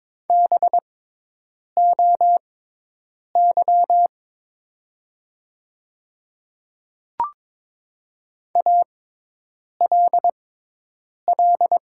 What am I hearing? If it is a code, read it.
Morse code, heard as a keyed tone, about 23 wpm, spaced as for 8 wpm, 700 Hz. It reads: BOY ALL
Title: 6x Character Spacing - Top 200 Words - 22wpm